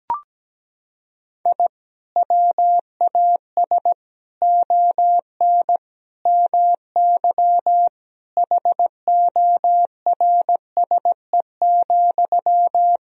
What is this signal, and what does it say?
Morse code, heard as a keyed tone, about 17 wpm, 700 Hz. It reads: I WAS ON MY HORSE,
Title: Sentences from Top 1000 Words - Review Part 3 - 17wpm